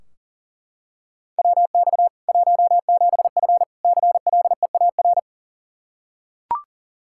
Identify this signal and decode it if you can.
Morse code, heard as a keyed tone, about 40 wpm, 700 Hz. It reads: WX 17F CLEAR
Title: Practice QSO - Exchange 2 - Info - 40wpm